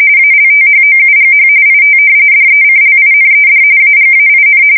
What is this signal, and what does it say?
RTTY, about 45.45 baud. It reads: ALTHOUGH IT DECLINED AGAIN
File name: fsk170 / usnavy5